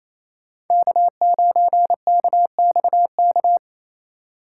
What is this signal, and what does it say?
Morse code, heard as a keyed tone, about 36 wpm, 700 Hz. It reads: K9KXK